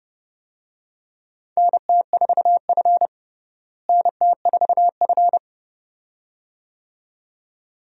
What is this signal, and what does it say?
Morse code, heard as a keyed tone, about 30 wpm, 700 Hz. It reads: NT4F NT4F
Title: USA Call Signs - 4 Characters - Repeated 2x - 30wpm